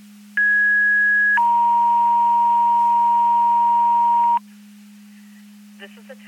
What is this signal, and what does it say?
Two-tone page, heard as a one-second tone, then a three-second tone, about 1650 then 940 Hz. It is heard from 0.4 to 4.4 s.
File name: Portage 154.385 Paging